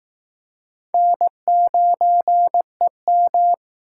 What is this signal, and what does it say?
Morse code, heard as a keyed tone, about 20 wpm, 700 Hz. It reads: N9EM